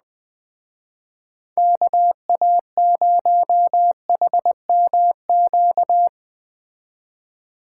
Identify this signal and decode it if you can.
Morse code, heard as a keyed tone, about 19 wpm, 700 Hz. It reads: KA0HMQ
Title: Speed Racing - USA 2x3 Call Signs 17wpm